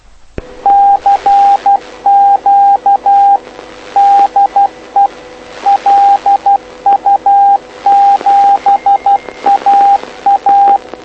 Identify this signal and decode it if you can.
Morse code, heard as a keyed tone, about 12 wpm, 800 Hz, with radio noise. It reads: CQ DE LU7AA
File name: lu4aao / lu7aa 09dic2011 exp planeador vuelo3